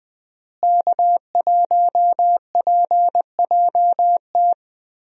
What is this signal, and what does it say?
Morse code, heard as a keyed tone, about 20 wpm, 700 Hz. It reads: K1PJT
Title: USA Call Signs - 5 Characters - Repeated 3x - 20wpm